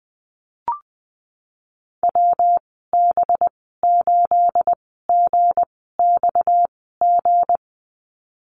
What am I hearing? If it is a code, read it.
Morse code, heard as a keyed tone, about 20 wpm, spaced as for 15 wpm, 700 Hz. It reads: WB8GXG
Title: 238 Lesson 38g - Letter q - Call Signs 20wpm